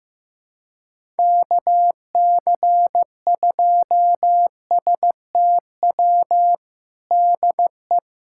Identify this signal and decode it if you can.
Morse code, heard as a keyed tone, about 15 wpm, 700 Hz. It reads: KC2STW DE
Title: USA Callsign DE Callsign - 6 Characters - 15wpm